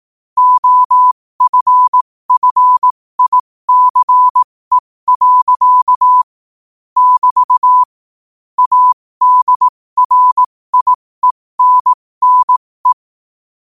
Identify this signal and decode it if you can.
Morse code, heard as a keyed tone, about 18 wpm, spaced as for 15 wpm, 1000 Hz. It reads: OFFICE. <BT> ADRIENNE
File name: qotdfiles / 15wpm